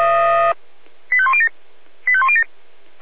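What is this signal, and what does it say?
Five-tone selective call (ZVEI), heard as quick runs of five tones, tones of about 70 ms. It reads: 85107 85107
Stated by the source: ZVEI five-tone call received by SDR.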